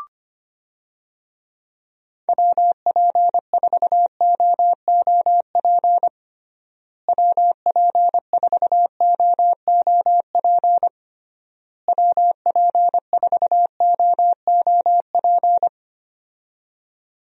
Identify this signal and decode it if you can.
Morse code, heard as a keyed tone, about 25 wpm, 700 Hz. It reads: WP4OOP WP4OOP WP4OOP
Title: USA Call Signs - 6 Characters - Repeated 3x - 25wpm